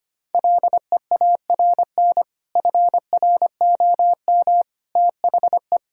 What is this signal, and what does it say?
Morse code, heard as a keyed tone, about 25 wpm, 700 Hz. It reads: LEARN FROM THE
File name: practice / 25wpm / Sentences with 2-Word Phrases - 25wpm